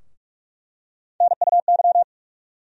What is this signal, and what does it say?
Morse code, heard as a keyed tone, about 47 wpm, 700 Hz. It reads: NUY
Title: USA Call Sign Postfixes - 3 Letters 45wpm